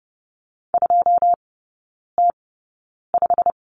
Morse code, heard as a keyed tone, about 30 wpm, spaced as for 9 wpm, 700 Hz. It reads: 2T5
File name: Lesson 18e - Letter m - Three Characters 30wpm